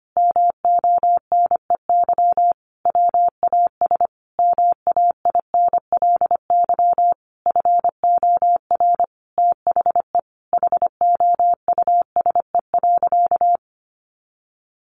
Morse code, heard as a keyed tone, about 25 wpm, 700 Hz. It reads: MONEY WAS MAINLY FOR THE HOUSE.
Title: Sentences from Top 1400 Words 25wpm